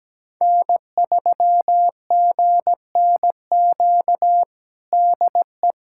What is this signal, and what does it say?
Morse code, heard as a keyed tone, about 17 wpm, 700 Hz. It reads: N3GNQ DE
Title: USA Callsign DE Callsign - 5 Characters - 17wpm